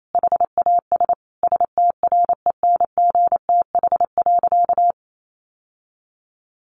Morse code, heard as a keyed tone, about 28 wpm, 700 Hz. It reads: HAS STRENGTH.
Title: Sentences from Top 1300 Words 28wpm